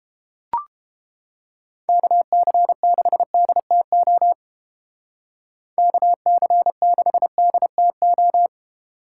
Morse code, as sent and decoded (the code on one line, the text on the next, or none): -.- -.-. -.... -.. - --- / -.- -.-. -.... -.. - ---
KC6DTO KC6DTO